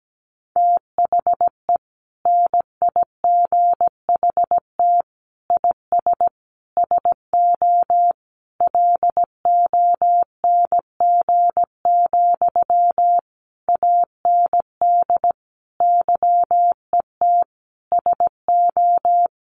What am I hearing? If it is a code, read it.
Morse code, heard as a keyed tone, about 17 wpm, 700 Hz. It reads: THE NIGHT IS SO LONG, AND YET SO
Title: Sentences from Top 500 Words 17wpm